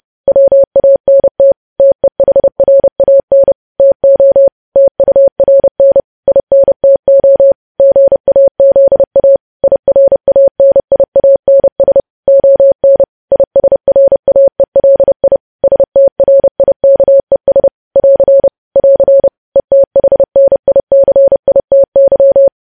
Morse code, heard as a keyed tone, about 30 wpm, 550 Hz. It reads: WANT TEHRAN TO TURN INTO GAZA IRANIANS ON ISRAELI STRIKES <AR> <AR> ETHNICITY
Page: news headlines in morse code at 30 wpm